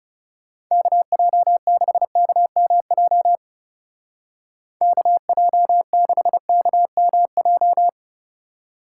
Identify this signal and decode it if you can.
Morse code, heard as a keyed tone, about 37 wpm, 700 Hz. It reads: KJ6KMJ KJ6KMJ
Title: Speed Racing - USA 2x3 Call Signs 30wpm